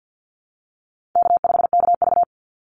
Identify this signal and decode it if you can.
Morse code, heard as a keyed tone, about 50 wpm, 700 Hz. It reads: K5KV
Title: Easy Call Signs - 50wpm